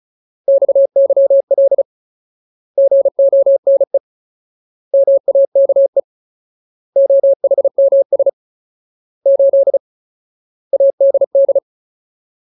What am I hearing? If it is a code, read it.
Morse code, heard as a keyed tone, about 34 wpm, 550 Hz. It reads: XYL GONE MAKE OHMS 8 ADD
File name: Ultimate Word List / 35WPM Ultimate